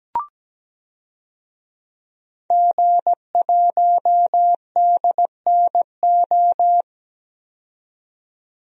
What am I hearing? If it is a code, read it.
Morse code, heard as a keyed tone, about 17 wpm, 700 Hz. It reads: G1DNO
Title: DXCC-100 Call Signs - 5 Characters - 17wpm